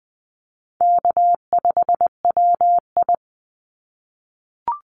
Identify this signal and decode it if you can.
Morse code, heard as a keyed tone, about 20 wpm, 700 Hz. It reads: K5WI
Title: NA SOTA Chaser Callsigns 20wpm